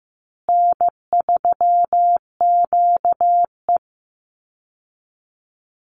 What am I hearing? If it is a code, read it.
Morse code, heard as a keyed tone, about 15 wpm, 700 Hz. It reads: N3QE
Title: Easy Call Signs Repeated 2x - 15wpm